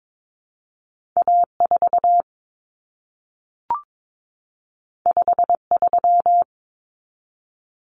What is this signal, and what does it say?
Morse code, heard as a keyed tone, about 22 wpm, 700 Hz. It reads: A4 53
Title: Common Cut Numbers - 2-Digit 22wpm